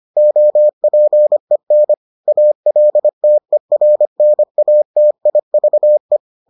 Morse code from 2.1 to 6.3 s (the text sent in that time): ALTERNATIVE